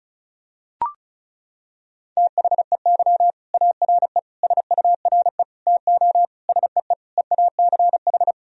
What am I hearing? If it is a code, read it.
Morse code, heard as a keyed tone, about 34 wpm, 700 Hz. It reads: THEY ARE SURE TO SEE EACH
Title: Sentences from Top 1000 Words - Review Part 1 - 35wpm